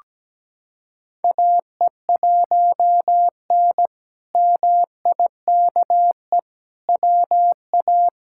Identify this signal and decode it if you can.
Morse code, heard as a keyed tone, about 17 wpm, 700 Hz. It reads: AE1N MIKE WA
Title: K1USN SST QSO Practice - 17wpm